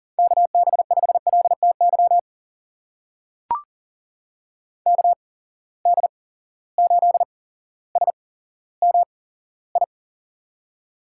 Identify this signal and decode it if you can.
Morse code, heard as a keyed tone, about 40 wpm, spaced as for 17 wpm, 700 Hz. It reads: KB5LTY KD8SMI
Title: USA Call Signs - 6 Characters - 8x Character Spacing - 40wpm